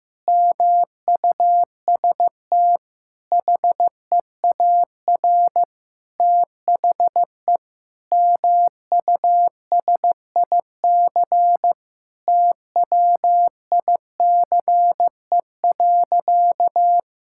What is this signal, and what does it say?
Morse code, heard as a keyed tone, about 15 wpm, 700 Hz. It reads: MUST HEAR THE MUSIC TWICE.
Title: Sentences from Top 1600 Words 15wpm